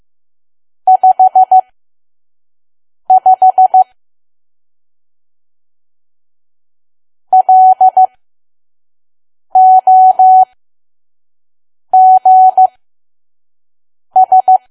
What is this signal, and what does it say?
Morse code, heard as a keyed tone, about 15 wpm, spaced as for 5 wpm, 750 Hz. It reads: 55 LOGS
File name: Archive / 5 WPM files / PM